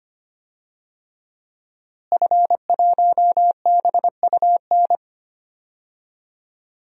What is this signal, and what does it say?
Morse code, heard as a keyed tone, about 25 wpm, 700 Hz. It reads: F1BUN
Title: DXCC-100 Call Signs - 5 Characters - Repeated 3x - 25wpm